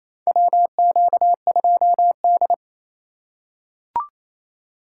Morse code, heard as a keyed tone, about 28 wpm, 700 Hz. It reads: WQ2D
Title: USA Call Signs - 4 Characters - 2x Character Spacing - 28wpm